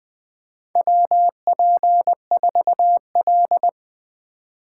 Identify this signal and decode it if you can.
Morse code, heard as a keyed tone, about 20 wpm, 700 Hz. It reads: WP4L
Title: USA Call Signs - 4x Character Spacing - 20wpm